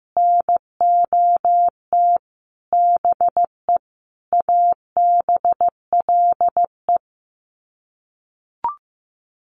Morse code, heard as a keyed tone, about 14 wpm, 700 Hz. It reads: NOT BE ABLE
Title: Sets of 5 Words Encore2 15wpm